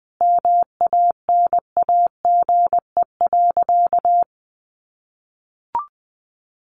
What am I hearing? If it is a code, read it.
Morse code, heard as a keyed tone, about 20 wpm, 700 Hz. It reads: MANAGE.
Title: Sentences from Top 1400 Words 20wpm